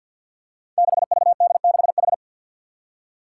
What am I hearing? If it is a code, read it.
Morse code, heard as a keyed tone, about 52 wpm, 700 Hz. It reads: 6VD6H
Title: Five Challenging Letter-Number Combinations 50wpm